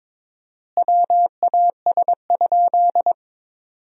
Morse code, heard as a keyed tone, about 22 wpm, 700 Hz. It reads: WAS?
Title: 4x Word-Spacing - Sentences from Top 200 Words - 22wpm